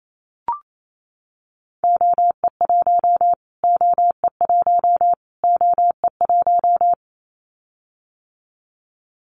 Morse code, heard as a keyed tone, about 28 wpm, 700 Hz. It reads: OE1 OE1 OE1
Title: Callsign Prefixes for Top 25 DXCC - Repeated 3x - 28wpm